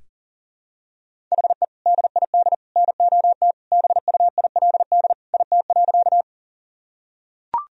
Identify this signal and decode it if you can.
Morse code, heard as a keyed tone, about 40 wpm, 700 Hz. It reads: HE DID NOT BUILD IT.
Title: Sentences from Top 1000 Words - Review Part 1 - 40wpm